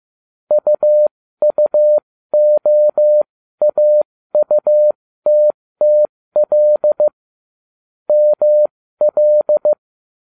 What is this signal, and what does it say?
Morse code, heard as a keyed tone, about 15 wpm, spaced as for 13 wpm, 600 Hz. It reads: UUOAUTTL ML